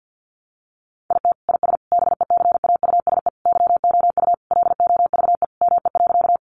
Morse code, heard as a keyed tone, about 50 wpm, 700 Hz. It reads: IT IS BECAUSE YOU LOVE ME.
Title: Sentences from Top 600 Words 50wpm